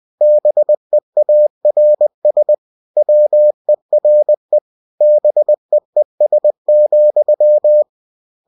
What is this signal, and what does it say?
Morse code, heard as a keyed tone, about 20 wpm, 600 Hz. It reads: BEARS WERE BEES,
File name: cw-books / winnie-the-pooh / winnie-20wpm-page-0003